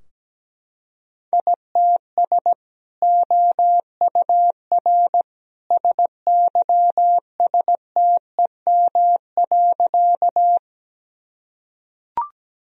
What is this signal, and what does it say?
Morse code, heard as a keyed tone, about 17 wpm, 700 Hz. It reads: ITS OUR SYSTEM.
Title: Sentences from Top 1000 Words - Review Part 3 - 17wpm